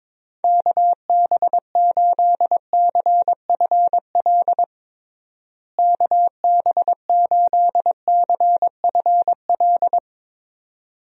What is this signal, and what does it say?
Morse code, heard as a keyed tone, about 22 wpm, 700 Hz. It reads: KB8CFL KB8CFL
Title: USA Call Signs - 6 Characters - Repeated 2x - 22wpm